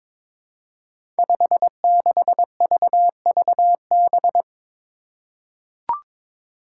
Morse code, heard as a keyed tone, about 22 wpm, 700 Hz. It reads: 56VVB
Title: Five Challenging Letter-Number Combinations 22wpm